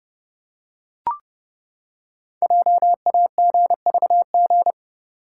Morse code, heard as a keyed tone, about 30 wpm, 700 Hz. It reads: JAGVG
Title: Code Groups - 5 Letters - 30wpm